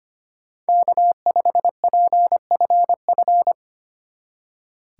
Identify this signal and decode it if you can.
Morse code, heard as a keyed tone, about 26 wpm, 700 Hz. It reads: K5PFF
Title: Speed Racing - USA Call Signs 25wpm